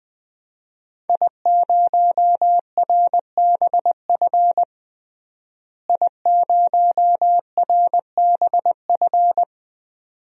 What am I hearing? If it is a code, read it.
Morse code, heard as a keyed tone, about 19 wpm, 700 Hz. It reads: I0RBF I0RBF
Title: DXCC-100 Call Signs - 5 Characters - Repeated 2x - 20wpm